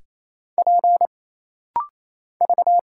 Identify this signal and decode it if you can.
Morse code, heard as a keyed tone, about 28 wpm, 700 Hz. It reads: P V
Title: Single Letters - Rapid-Fire 28wpm